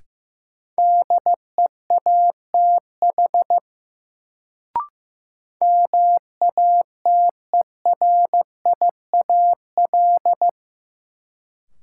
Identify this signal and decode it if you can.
Morse code, heard as a keyed tone, about 15 wpm, 700 Hz. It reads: DEATH MATERIAL
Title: Sentences from Top 1100 Words - New Word Drill Alt - 15wpm